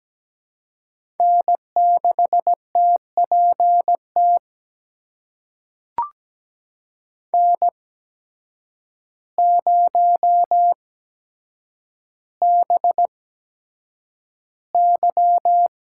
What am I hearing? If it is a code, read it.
Morse code, heard as a keyed tone, about 17 wpm, spaced as for 8 wpm, 700 Hz. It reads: N6TPT N0BY
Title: USA Call Signs - 8x Character Spacing - 17wpm